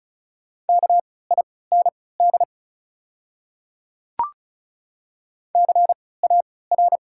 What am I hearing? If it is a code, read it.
Morse code, heard as a keyed tone, about 34 wpm, spaced as for 20 wpm, 690 Hz. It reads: KIND CAR
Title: 3x Character Spacing - Top 300 Words - 35wpm